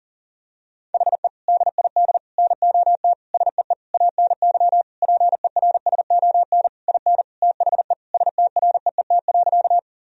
Morse code, heard as a keyed tone, about 40 wpm, 700 Hz. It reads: HE DID NOT SEE ANY PERSON IN THE STREET.
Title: Sentences from Top 1000 Words - Review Part 3 - 40wpm